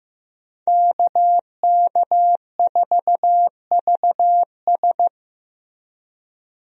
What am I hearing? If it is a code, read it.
Morse code, heard as a keyed tone, about 15 wpm, 700 Hz. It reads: KK4VS
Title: USA Call Signs - 5 Characters - Repeated 3x - 15wpm